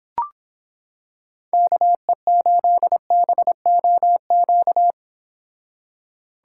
Morse code, heard as a keyed tone, about 26 wpm, 700 Hz. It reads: KE8BOQ